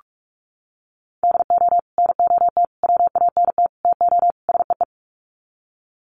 Morse code, heard as a keyed tone, about 44 wpm, 700 Hz. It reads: DO NOT WANT TO SEE
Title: Sets of 5 Words 45wpm